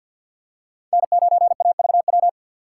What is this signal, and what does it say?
Morse code, heard as a keyed tone, about 47 wpm, 690 Hz. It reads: N9AVW
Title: USA Call Signs - 6x Character Spacing - 50wpm